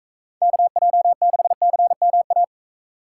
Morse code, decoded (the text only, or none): KJ6CMA